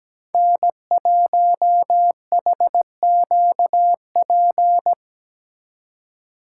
Morse code, heard as a keyed tone, about 20 wpm, 700 Hz. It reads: N1HQP
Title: Speed Racing - USA 1x3 Call Signs 17wpm